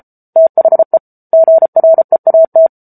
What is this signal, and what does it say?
Morse code, heard as a keyed tone, about 34 wpm, 650 Hz. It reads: THE GREAT
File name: Book Two / 07-Chapter 7 - 33 WPM